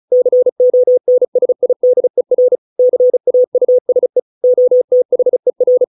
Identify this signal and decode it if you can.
Morse code, heard as a keyed tone, about 34 wpm, 500 Hz. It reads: CONSIDER CAUSE OTHER